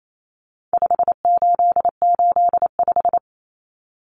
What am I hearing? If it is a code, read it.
Morse code, heard as a keyed tone, about 28 wpm, 700 Hz. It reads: 5885
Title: Numbers - 4-Digit 28wpm